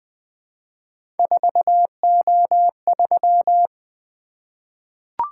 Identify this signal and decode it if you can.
Morse code, heard as a keyed tone, about 20 wpm, 700 Hz. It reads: 4O3
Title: Three Letter-Number Combinations - 20wpm